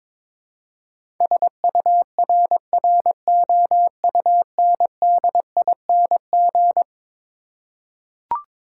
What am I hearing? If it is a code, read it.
Morse code, heard as a keyed tone, about 22 wpm, 700 Hz. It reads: SURROUNDING
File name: practice / 22wpm / Eleven Letter Words - 22wpm